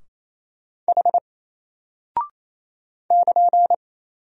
Morse code, heard as a keyed tone, about 28 wpm, 700 Hz. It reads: H <KN>
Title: Single Character (Letter:Number:Punctuation:Prosign) Alt 28wpm